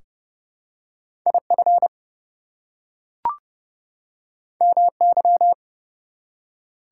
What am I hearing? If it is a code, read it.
Morse code, heard as a keyed tone, about 29 wpm, 700 Hz. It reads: IF MY
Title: Two Letter Words - 30wpm